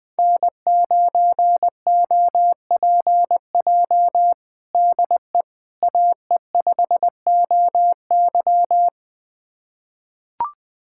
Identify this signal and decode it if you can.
Morse code, heard as a keyed tone, about 20 wpm, 700 Hz. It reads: N9OPJ DE AE5OY
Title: USA Callsign DE Callsign - 5 Characters - 20wpm